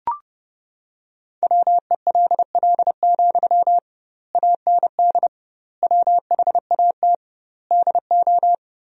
Morse code, heard as a keyed tone, about 30 wpm, 700 Hz. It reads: WELL, AND WHAT DO
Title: 2x Word-Spacing - Sentences from Top 200 Words - 30wpm